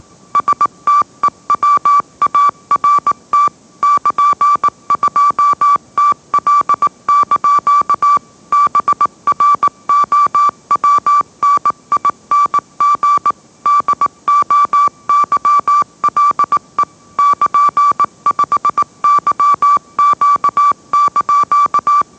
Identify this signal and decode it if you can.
Morse code, heard as a keyed tone, about 18 wpm, 1160 Hz. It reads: STEWART <KN>2TL) BROWNING DOYLE <KN>5YQ)
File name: mrx / mrx082 20wpm